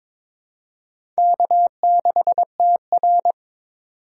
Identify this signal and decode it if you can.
Morse code, heard as a keyed tone, about 22 wpm, 700 Hz. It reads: K6TR
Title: USA Call Signs - 4 Characters - 6x Character Spacing - 22wpm